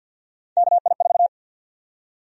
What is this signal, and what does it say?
Morse code, heard as a keyed tone, about 47 wpm, 700 Hz. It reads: KI4
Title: USA Call Sign Prefixes - Repeated 2x - 50wpm